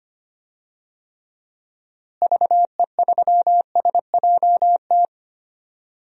Morse code, heard as a keyed tone, about 25 wpm, 700 Hz. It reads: VE3SJT
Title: DXCC-100 Call Signs - 6 Characters - Repeated 3x - 25wpm